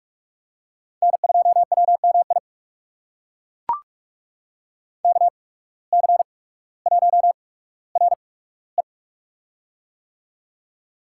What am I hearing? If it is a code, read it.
Morse code, heard as a keyed tone, about 41 wpm, spaced as for 19 wpm, 700 Hz. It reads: N2WMI KC1RE
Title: USA Call Signs - 5 Characters - 8x Character Spacing - 45wpm